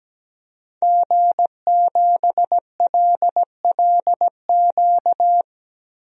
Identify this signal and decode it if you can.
Morse code, heard as a keyed tone, about 17 wpm, 700 Hz. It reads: G7LLQ
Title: DXCC-100 Call Signs - 5 Characters - 17wpm